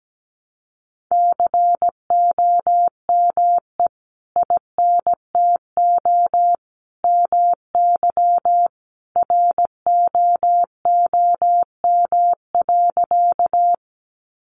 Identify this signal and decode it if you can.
Morse code, heard as a keyed tone, about 17 wpm, 700 Hz. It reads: COME INTO MY ROOM.
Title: Sentences from Top 600 Words 17wpm